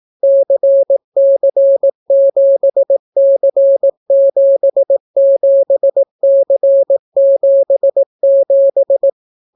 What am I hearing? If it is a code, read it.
Morse code, heard as a keyed tone, about 18 wpm, 550 Hz. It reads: CC7C77C77